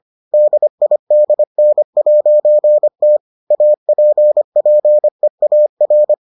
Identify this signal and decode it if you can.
Morse code, heard as a keyed tone, about 25 wpm, 600 Hz. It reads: DIDN'T APPEAR